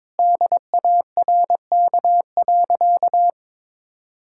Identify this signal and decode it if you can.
Morse code, heard as a keyed tone, about 22 wpm, 700 Hz. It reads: DARK.
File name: practice / 22wpm / Sentences with 4-Word Phrases - 22wpm